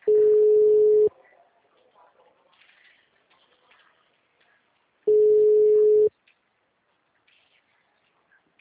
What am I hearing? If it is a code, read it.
Ringback tone (Europe, 425 Hz).
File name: out- -2029- - - .53813